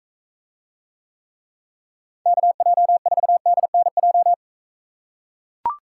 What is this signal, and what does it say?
Morse code, heard as a keyed tone, about 40 wpm, 700 Hz. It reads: KJ4DNJ